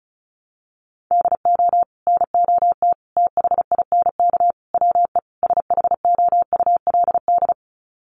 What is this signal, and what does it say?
Morse code, heard as a keyed tone, about 34 wpm, 700 Hz. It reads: DO NOT THINK WE SHOULD
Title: Sets of 5 Words Encore 35wpm